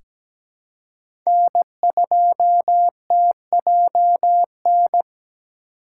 Morse code, heard as a keyed tone, about 17 wpm, 700 Hz. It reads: N2TJN
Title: USA Call Signs - 5 Characters - 4x Character Spacing - 17wpm